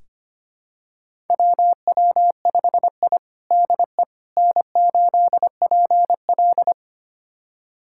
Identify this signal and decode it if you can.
Morse code, heard as a keyed tone, about 25 wpm, 700 Hz. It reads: WW5I DE N8PL
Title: USA Callsign DE Callsign - 4 Characters - 25wpm